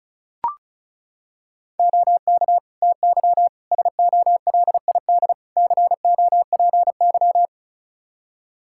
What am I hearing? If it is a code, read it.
Morse code, heard as a keyed tone, about 36 wpm, 700 Hz. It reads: OK TY SOLID COPY
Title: K1USN SST QSO Practice 35wpm